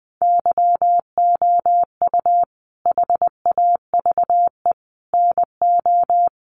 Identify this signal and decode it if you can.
Morse code, heard as a keyed tone, about 20 wpm, 700 Hz. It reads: YOU HAVE NO
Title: Sentences from Top 1100 Words 20wpm